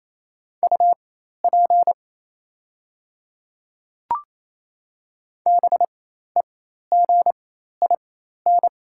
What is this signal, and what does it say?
Morse code, heard as a keyed tone, about 27 wpm, spaced as for 13 wpm, 690 Hz. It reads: UP BEGIN
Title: 4x Character Spacing - Top 300 Words - 28wpm